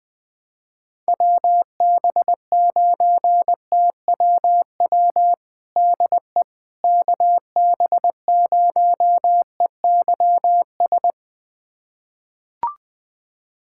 Morse code, heard as a keyed tone, about 20 wpm, 700 Hz. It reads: WB9TWW DE KB0EYS
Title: USA Callsign DE Callsign - 6 Characters - 20wpm